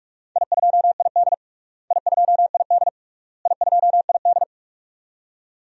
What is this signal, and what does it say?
Morse code, heard as a keyed tone, about 46 wpm, 700 Hz. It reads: I2ID I2ID I2ID
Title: DXCC-100 Call Signs - 4 Characters - Repeated 3x - 45wpm